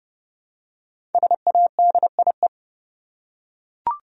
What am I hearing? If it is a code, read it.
Morse code, heard as a keyed tone, about 30 wpm, 700 Hz. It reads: SADIE
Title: USA Common Names - Max 5 Letters - 30wpm